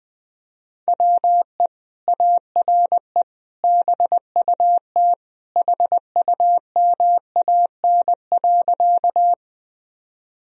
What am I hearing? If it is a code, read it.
Morse code, heard as a keyed tone, about 20 wpm, 700 Hz. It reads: WE ARE BUT HUMAN.
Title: Sentences from Top 1000 Words - Review Part 3 - 20wpm